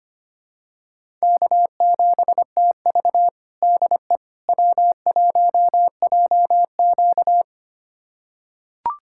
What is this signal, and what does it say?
Morse code, heard as a keyed tone, about 25 wpm, 700 Hz. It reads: K7TV DE W1JQ
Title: USA Callsign DE Callsign - 4 Characters - 25wpm